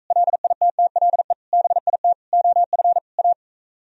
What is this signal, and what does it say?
Morse code, heard as a keyed tone, about 45 wpm, 700 Hz. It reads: LITTLE BIT OF A